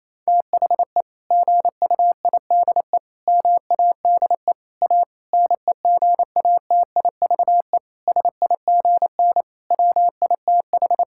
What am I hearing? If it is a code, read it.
Morse code, heard as a keyed tone, about 27 wpm, 700 Hz. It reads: THE GUIDE MADE A NEGATIVE SIGN WITH
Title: Sentences from Top 1000 Words - Review Part 6 - 28wpm